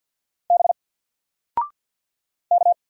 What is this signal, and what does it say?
Morse code, heard as a keyed tone, about 50 wpm, 700 Hz. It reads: B K
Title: Single Letter-Number 50wpm